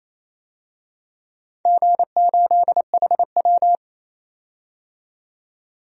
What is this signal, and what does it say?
Morse code, heard as a keyed tone, about 28 wpm, 700 Hz. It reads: G8HW
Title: DXCC-100 Call Signs - 4 Characters - 28wpm